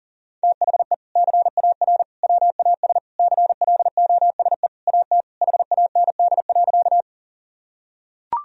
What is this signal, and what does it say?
Morse code, heard as a keyed tone, about 40 wpm, 700 Hz. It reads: THE CAR WAS CLOSE AT HAND.
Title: Sentences from Top 1000 Words - Review Part 1 - 40wpm